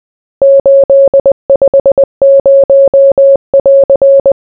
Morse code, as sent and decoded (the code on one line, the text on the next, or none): ---.. ..... ----- .-.-.
850<AR>